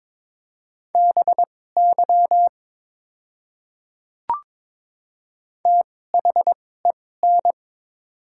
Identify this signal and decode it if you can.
Morse code, heard as a keyed tone, about 22 wpm, spaced as for 16 wpm, 690 Hz. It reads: BY THEN
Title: 2x Character Spacing - Top 100 Words - 22wpm